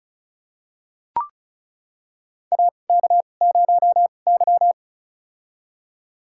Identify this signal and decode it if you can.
Morse code, heard as a keyed tone, about 34 wpm, spaced as for 25 wpm, 700 Hz. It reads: AK0Y
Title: USA Call Signs - 2x Character Spacing - 35wpm